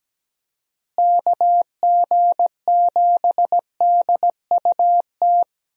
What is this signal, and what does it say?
Morse code, heard as a keyed tone, about 17 wpm, 700 Hz. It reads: KG7DUT
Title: USA Call Signs - 6 Characters - Repeated 3x - 17wpm